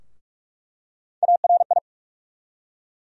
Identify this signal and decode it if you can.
Morse code, heard as a keyed tone, about 46 wpm, 700 Hz. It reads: ARI
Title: USA Call Sign Postfixes - 3 Letters 45wpm